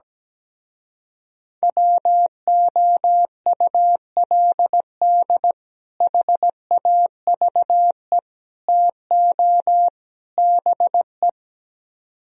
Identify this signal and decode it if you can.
Morse code, heard as a keyed tone, about 17 wpm, 700 Hz. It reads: WOULD HAVE TO BE
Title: Speed-Racing - Sets of 4 Words 17wpm